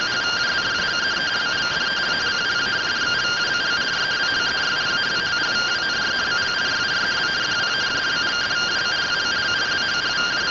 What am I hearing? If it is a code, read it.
Two-tone teleprinter signal, about 170 Hz shift.